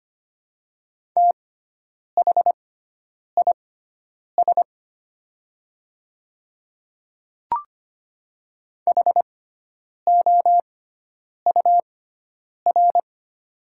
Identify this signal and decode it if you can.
Morse code, heard as a keyed tone, about 25 wpm, spaced as for 9 wpm, 700 Hz. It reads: THIS HOUR
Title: 6x Character Spacing - Top 500 Words - 25wpm